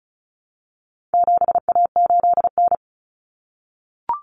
Morse code, heard as a keyed tone, about 34 wpm, 700 Hz. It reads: 7A8N
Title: Common Cut Numbers - 4-Digit 35wpm